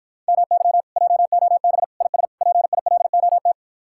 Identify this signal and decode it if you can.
Morse code, heard as a keyed tone, about 52 wpm, 700 Hz. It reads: MY JOB IS PILOT